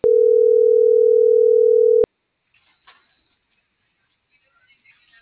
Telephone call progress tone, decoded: ringback tone

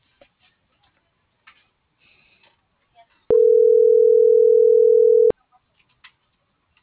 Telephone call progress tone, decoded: ringback tone

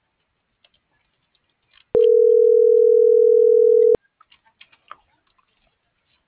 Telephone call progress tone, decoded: ringback tone

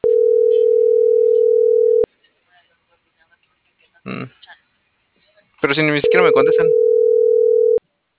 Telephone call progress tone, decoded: ringback tone

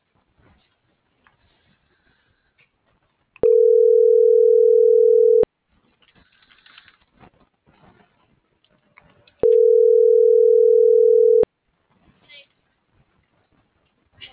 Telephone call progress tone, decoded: ringback tone

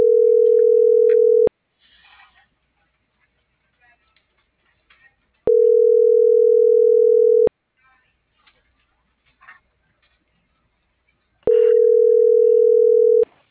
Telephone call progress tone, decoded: ringback tone

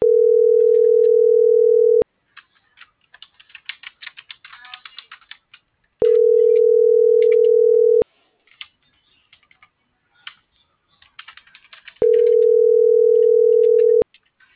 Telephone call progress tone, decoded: ringback tone

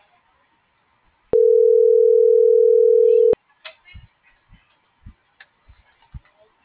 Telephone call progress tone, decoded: ringback tone

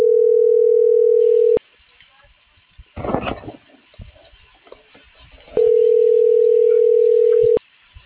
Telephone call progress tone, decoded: ringback tone

